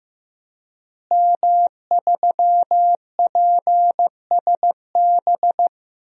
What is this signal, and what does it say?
Morse code, heard as a keyed tone, about 15 wpm, 700 Hz. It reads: M3PSB
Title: DXCC-100 Call Signs - 5 Characters - Repeated 3x - 15wpm